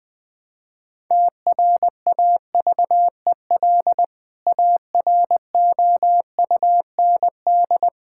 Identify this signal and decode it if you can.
Morse code, heard as a keyed tone, about 21 wpm, 700 Hz. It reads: TRAVEL AROUND
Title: Sentences with 3-Word Phrases - 20wpm